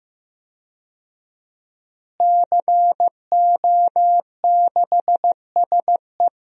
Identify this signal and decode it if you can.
Morse code, heard as a keyed tone, about 15 wpm, 700 Hz. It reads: CO6SE
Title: DXCC-100 Call Signs - 5 Characters - Repeated 2x - 15wpm